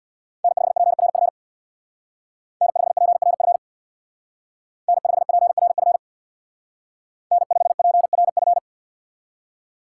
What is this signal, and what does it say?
Morse code, heard as a keyed tone, about 64 wpm, 700 Hz. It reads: N5PRF N5PRF N5PRF N5PRF